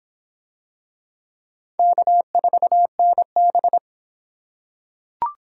Morse code, heard as a keyed tone, about 25 wpm, 700 Hz. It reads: K4NB